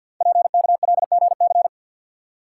Morse code, heard as a keyed tone, about 52 wpm, 700 Hz. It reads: PKLGC
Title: Code Groups - 5 Letter:Numbers - 50wpm